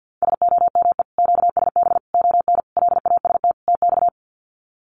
Morse code, heard as a keyed tone, about 50 wpm, 700 Hz. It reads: SOME QSB ON LAST TX
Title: Common QSO Phrases 50wpm